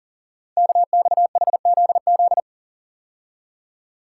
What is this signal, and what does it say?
Morse code, heard as a keyed tone, about 39 wpm, 690 Hz. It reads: KXHZZ
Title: Code Groups - 5 Letters - 40wpm